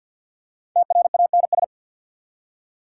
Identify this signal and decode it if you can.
Morse code, heard as a keyed tone, about 51 wpm, 690 Hz. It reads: TRANS
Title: Prefixes and Suffixes 50wpm